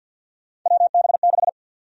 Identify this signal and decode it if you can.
Morse code, heard as a keyed tone, about 50 wpm, 700 Hz. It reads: WB6
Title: USA Call Sign Prefixes - 50wpm